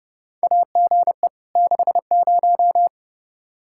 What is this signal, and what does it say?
Morse code, heard as a keyed tone, about 30 wpm, 700 Hz. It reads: AGE 60
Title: Practice QSO - Exchange 3 - More Info - 30wpm